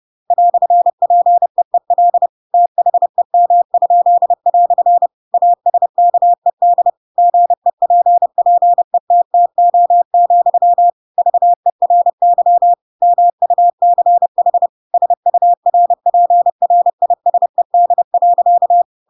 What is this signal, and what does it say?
Morse code, heard as a keyed tone, about 30 wpm, 700 Hz. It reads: "PEEL THEM?" ASKED GEPPETTO, VERY MUCH SURPRISED.
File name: Pinocchio / Pinocchio ch7 30 30 ik1hge